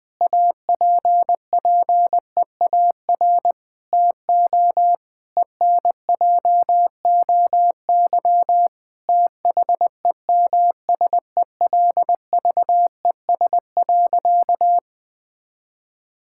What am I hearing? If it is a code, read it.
Morse code, heard as a keyed tone, about 20 wpm, 700 Hz. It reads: APPEAR TO ENJOY THEMSELVES.
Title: Sentences from Top 1000 Words - Review Part 5 - 20wpm